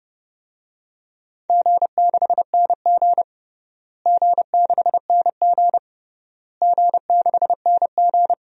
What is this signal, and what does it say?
Morse code, heard as a keyed tone, about 32 wpm, 700 Hz. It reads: G6NG G6NG G6NG
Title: DXCC-100 Call Signs - 4 Characters - Repeated 3x - 30wpm